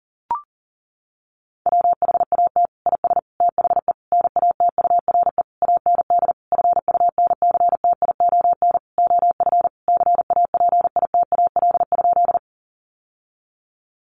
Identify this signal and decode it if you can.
Morse code, heard as a keyed tone, about 40 wpm, 700 Hz. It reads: WHAT IS THE NATURE AND FUNCTION OF CAPITAL?
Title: Sentences from Top 1100 Words 40wpm